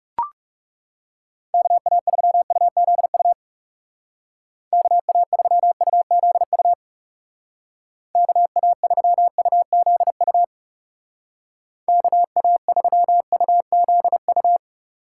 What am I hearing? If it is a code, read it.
Morse code, heard as a keyed tone, about 39 wpm, 700 Hz. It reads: KA3UZU KA3UZU KA3UZU KA3UZU